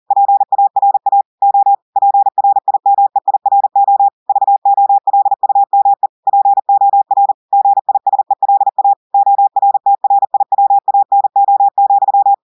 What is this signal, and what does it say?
Morse code, heard as a keyed tone, about 40 wpm, 800 Hz. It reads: PARA O PRIMEIRO VOLUME POR GISELA ORTRIWANO,
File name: audios / portugais / portugais texte en clair 1 vitesse 40 mots minutes